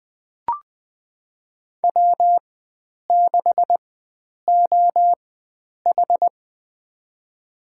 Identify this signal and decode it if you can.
Morse code, heard as a keyed tone, about 20 wpm, spaced as for 9 wpm, 700 Hz. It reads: W6OH
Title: USA Call Signs - 4 Characters - 4x Character Spacing - 20wpm